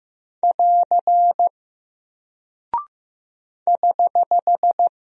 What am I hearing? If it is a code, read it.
Morse code, heard as a keyed tone, about 15 wpm, 700 Hz. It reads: <AR> <HH>
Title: Single Character (Letter:Number:Punctuation:Prosign) Alt 15wpm